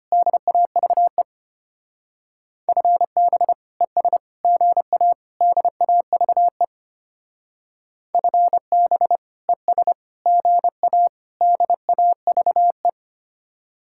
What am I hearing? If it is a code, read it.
Morse code, heard as a keyed tone, about 31 wpm, 700 Hz. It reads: DAVE FB ES GA DAVE FB ES GA DAVE
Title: Speed-Racing Common QSO Phrases Encore - 25wpm